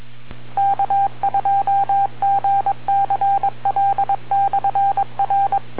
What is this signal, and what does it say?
Morse code, heard as a keyed tone, about 22 wpm, 790 Hz. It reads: K2GCL/R